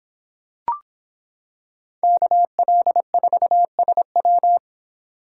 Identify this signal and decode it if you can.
Morse code, heard as a keyed tone, about 26 wpm, 700 Hz. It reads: KL4SW